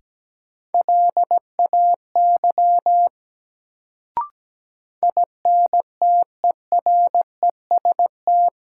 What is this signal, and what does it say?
Morse code, heard as a keyed tone, about 17 wpm, 700 Hz. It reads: LAY INTEREST
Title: Sentences from Top 400 Words - New Word Drill - 17wpm